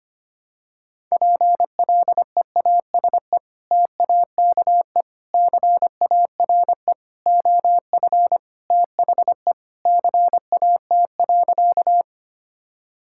Morse code, heard as a keyed tone, about 25 wpm, 700 Hz. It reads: PLEASE TAKE CARE OF THE CAT.
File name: practice / 25wpm / Sentences with 3-Word Phrases - 25wpm